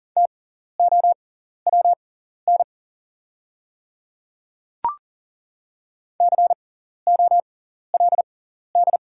Morse code, heard as a keyed tone, about 40 wpm, spaced as for 14 wpm, 690 Hz. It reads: TOWN COLD
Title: 6x Character Spacing - Top 500 Words - 40wpm